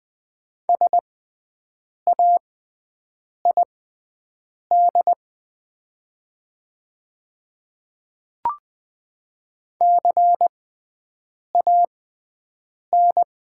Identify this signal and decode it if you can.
Morse code, heard as a keyed tone, about 20 wpm, spaced as for 7 wpm, 700 Hz. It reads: SAID CAN
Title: 6x Character Spacing - Top 100 Words - 20wpm